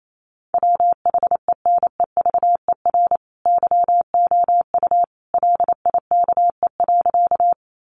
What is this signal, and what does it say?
Morse code, heard as a keyed tone, about 28 wpm, 700 Hz. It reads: WHENEVER YOU LIKE.
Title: Sentences from Top 1200 Words 28wpm